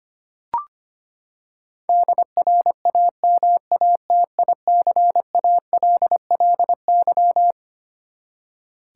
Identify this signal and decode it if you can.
Morse code, heard as a keyed tone, about 25 wpm, 700 Hz. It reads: DRAMATICALLY
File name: practice / 25wpm / Twelve Letter Words - 25wpm